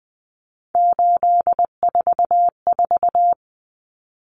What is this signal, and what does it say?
Morse code, heard as a keyed tone, about 20 wpm, 700 Hz. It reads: 844
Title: Common Cut Numbers - 3-Digit 20wpm